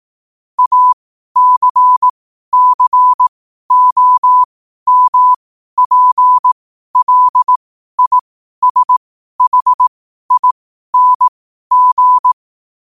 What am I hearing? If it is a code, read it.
Morse code, heard as a keyed tone, about 18 wpm, spaced as for 12 wpm, 1000 Hz. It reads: ACCOMPLISHING
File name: qotdfiles / 12wpm